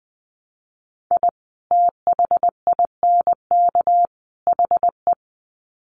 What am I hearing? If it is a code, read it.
Morse code, heard as a keyed tone, about 20 wpm, 700 Hz. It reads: I THINK HE
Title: Sets of 3 Words 20wpm